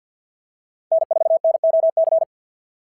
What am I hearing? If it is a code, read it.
Morse code, heard as a keyed tone, about 50 wpm, 660 Hz. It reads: N4NOC
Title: SOTA QSO Practice 50wpm